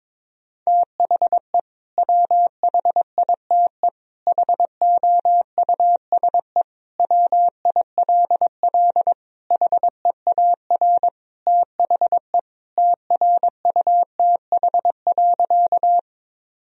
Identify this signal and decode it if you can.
Morse code, heard as a keyed tone, about 22 wpm, 700 Hz. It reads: THE WHITE HOUSE WILL HEAR THE TRUTH.
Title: Sentences from Top 1000 Words - Review Part 1 - 22wpm